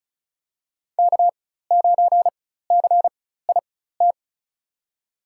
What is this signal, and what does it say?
Morse code, heard as a keyed tone, about 36 wpm, spaced as for 16 wpm, 700 Hz. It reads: K9CIT
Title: USA Call Signs - 5 Characters - 4x Character Spacing - 35wpm